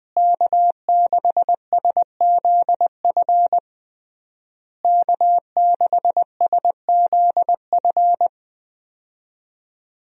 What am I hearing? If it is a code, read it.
Morse code, heard as a keyed tone, about 21 wpm, 700 Hz. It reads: K6SZF K6SZF
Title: USA Call Signs - 5 Characters - Repeated 3x - 20wpm